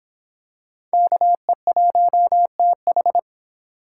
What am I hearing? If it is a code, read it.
Morse code, heard as a keyed tone, about 26 wpm, 700 Hz. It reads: KE1TH